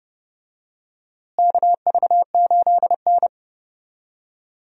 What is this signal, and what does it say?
Morse code, heard as a keyed tone, about 30 wpm, 700 Hz. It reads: KV8N
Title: USA Call Signs - 4 Characters - Repeated 3x - 30wpm